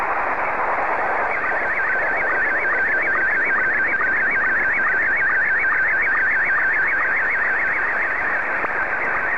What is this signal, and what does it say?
SSTV (slow-scan television), Scottie 1.